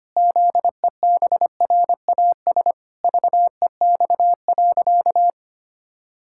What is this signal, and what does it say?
Morse code, heard as a keyed tone, about 25 wpm, 700 Hz. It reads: ZEBRAS VEX.
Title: Pangram Challenge v2 - 25wpm